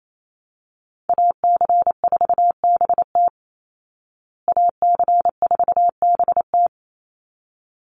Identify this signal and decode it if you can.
Morse code, heard as a keyed tone, about 28 wpm, 700 Hz. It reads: AC4BT AC4BT
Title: Call Signs Repeated 3x - 28wpm